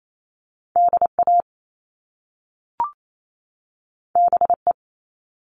Morse code, heard as a keyed tone, about 28 wpm, 700 Hz. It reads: DA BE
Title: Top Two Letter Combinations in Words 28wpm